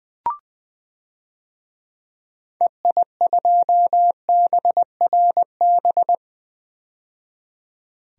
Morse code, heard as a keyed tone, about 21 wpm, 700 Hz. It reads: EI2BRB
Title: DXCC-100 Call Signs - 6 Characters - 20wpm